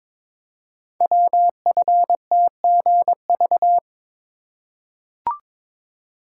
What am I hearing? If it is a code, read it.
Morse code, heard as a keyed tone, about 22 wpm, 700 Hz. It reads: WFTGV
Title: Code Groups - 5 Letters - 22wpm